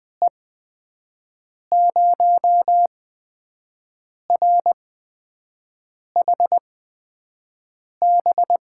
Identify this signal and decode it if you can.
Morse code, heard as a keyed tone, about 21 wpm, spaced as for 5 wpm, 700 Hz. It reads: E0RHB
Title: USA Call Signs - 6 Characters - 8x Character Spacing - 20wpm